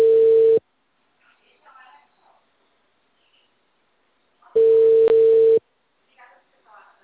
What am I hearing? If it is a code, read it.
Ringback tone (China, 450 Hz), with speech.